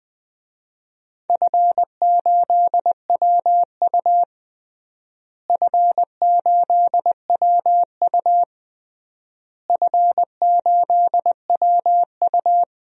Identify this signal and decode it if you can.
Morse code, heard as a keyed tone, about 21 wpm, 700 Hz. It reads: F8WU F8WU F8WU
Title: DXCC-100 Call Signs - 4 Characters - Repeated 3x - 20wpm